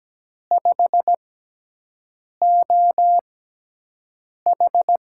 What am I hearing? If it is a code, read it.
Morse code, heard as a keyed tone, about 17 wpm, spaced as for 6 wpm, 700 Hz. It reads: 5OH
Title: USA Call Signs - 6 Characters - 6x Character Spacing - 17wpm